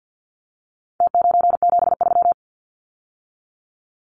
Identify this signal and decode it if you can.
Morse code, heard as a keyed tone, about 50 wpm, 700 Hz. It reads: T973
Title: Numbers - Contest Style - 4-Digit 50wpm